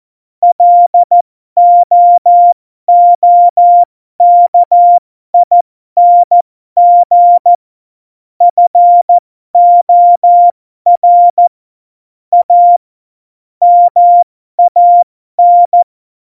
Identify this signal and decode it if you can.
Morse code, heard as a keyed tone, about 14 wpm, spaced as for 12 wpm, 700 Hz. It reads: LOOKING FOR A MAN
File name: Morsekurs / Witze